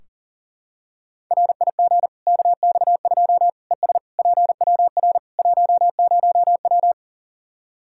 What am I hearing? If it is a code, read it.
Morse code, heard as a keyed tone, about 40 wpm, 700 Hz. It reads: RIG KX2 ES PWR 10W
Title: K1USN SST QSO Practice 40wpm